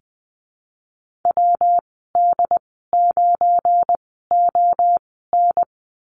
Morse code, heard as a keyed tone, about 19 wpm, spaced as for 14 wpm, 700 Hz. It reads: WD9ON
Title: 179 Lesson 29g - Slash - Call Signs 20wpm